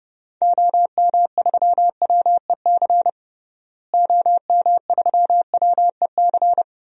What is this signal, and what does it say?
Morse code, heard as a keyed tone, about 30 wpm, 700 Hz. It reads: OM3WEC OM3WEC
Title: DXCC-100 Call Signs - 6 Characters - Repeated 2x - 30wpm